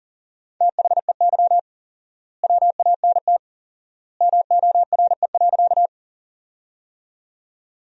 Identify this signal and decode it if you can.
Morse code, heard as a keyed tone, about 40 wpm, 700 Hz. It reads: THEY WANT MORE.
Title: 4x Word-Spacing - Sentences from Top 200 Words - 40wpm